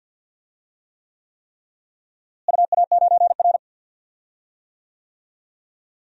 Morse code, heard as a keyed tone, about 47 wpm, 700 Hz. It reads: UA9R
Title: DXCC-100 Call Signs - 4 Characters - Repeated 3x - 50wpm